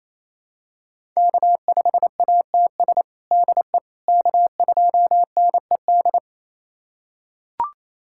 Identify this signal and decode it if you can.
Morse code, heard as a keyed tone, about 28 wpm, 700 Hz. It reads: K5ATS DE K2NED
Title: USA Callsign DE Callsign - 5 Characters - 28wpm